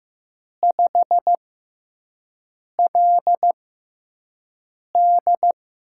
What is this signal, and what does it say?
Morse code, heard as a keyed tone, about 15 wpm, spaced as for 5 wpm, 700 Hz. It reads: 5LD
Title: USA Call Signs - 4 Characters - 6x Character Spacing - 15wpm